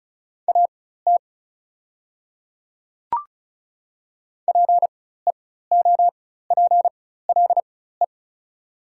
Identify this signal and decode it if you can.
Morse code, heard as a keyed tone, about 36 wpm, spaced as for 16 wpm, 700 Hz. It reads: AT PEOPLE
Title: 4x Character Spacing - Top 100 Words - 35wpm